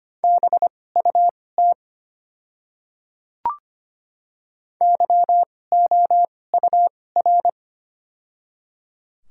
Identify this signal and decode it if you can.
Morse code, heard as a keyed tone, about 25 wpm, spaced as for 18 wpm, 700 Hz. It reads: BUT YOUR
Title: 2x Character Spacing - Top 200 Words - 25wpm